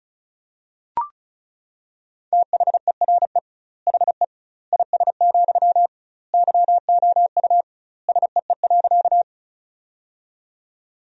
Morse code, heard as a keyed tone, about 34 wpm, 700 Hz. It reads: THERE HE IS, YOU SEE.
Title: 2x Word-Spacing - Sentences from Top 100 Words - 35wpm